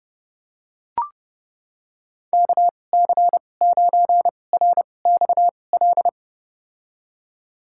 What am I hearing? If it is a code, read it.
Morse code, heard as a keyed tone, about 32 wpm, spaced as for 22 wpm, 700 Hz. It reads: KC9RXL
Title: USA Call Signs - 6 Characters - 2x Character Spacing - 30wpm